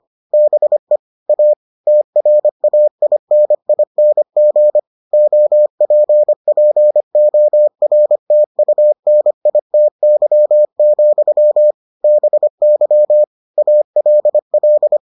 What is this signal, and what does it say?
Morse code, heard as a keyed tone, about 25 wpm, 600 Hz. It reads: BE A TRAINING OPPORTUNITY, BY ALL